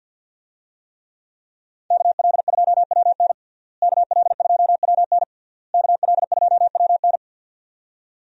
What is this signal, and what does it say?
Morse code, heard as a keyed tone, about 52 wpm, 700 Hz. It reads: KL2WN KL2WN KL2WN
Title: USA Call Signs - 5 Characters - Repeated 3x - 50wpm